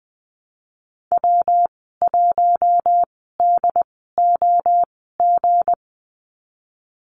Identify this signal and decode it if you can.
Morse code, heard as a keyed tone, about 20 wpm, spaced as for 14 wpm, 700 Hz. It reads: W1DOG
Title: 179 Lesson 29g - Slash - Call Signs 20wpm